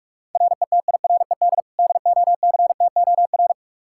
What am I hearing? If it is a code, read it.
Morse code, heard as a keyed tone, about 45 wpm, 700 Hz. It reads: RETIRED DOCTOR
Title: K1USN SST QSO Practice 45wpm